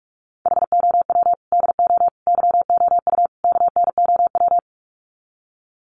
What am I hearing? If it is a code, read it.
Morse code, heard as a keyed tone, about 44 wpm, 700 Hz. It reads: HOW DO YOU KNOW
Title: Sets of 4 Words 45wpm